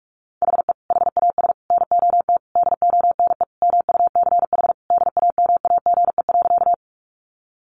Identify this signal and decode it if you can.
Morse code, heard as a keyed tone, about 44 wpm, 700 Hz. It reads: HE HAS NOT DONE MUCH DAMAGE.
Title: Sentences from Top 1100 Words 45wpm